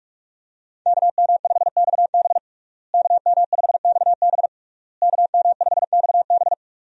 Morse code, heard as a keyed tone, about 41 wpm, 700 Hz. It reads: KM5XB KM5XB KM5XB
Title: USA Call Signs - 5 Characters - Repeated 3x - 45wpm